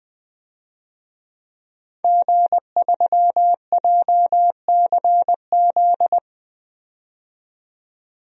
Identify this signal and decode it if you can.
Morse code, heard as a keyed tone, about 20 wpm, 700 Hz. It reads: G3JCZ
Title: DXCC-100 Call Signs - 5 Characters - 20wpm